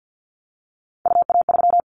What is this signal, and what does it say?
Morse code, heard as a keyed tone, about 50 wpm, 700 Hz. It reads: UA3
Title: Callsign Prefixes for Top 100 DXCC - 50wpm